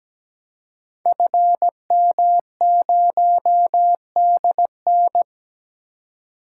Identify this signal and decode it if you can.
Morse code, heard as a keyed tone, about 17 wpm, 700 Hz. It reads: FM0DN
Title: Code Groups - 5 Letter:Numbers - 17wpm